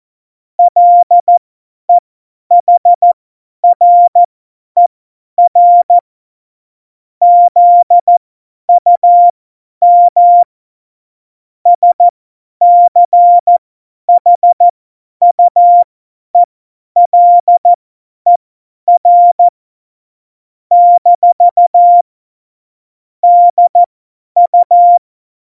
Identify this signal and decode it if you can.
Morse code, heard as a keyed tone, about 14 wpm, spaced as for 10 wpm, 700 Hz. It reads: LEHRER ZUM SCHUELER - DU
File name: Morsekurs / Witze / Witz 14. Mai 2019, 50 F70